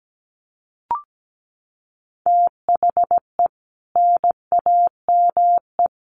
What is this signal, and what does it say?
Morse code, heard as a keyed tone, about 17 wpm, 700 Hz. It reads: THE NAME
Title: Sets of 4 Words 17wpm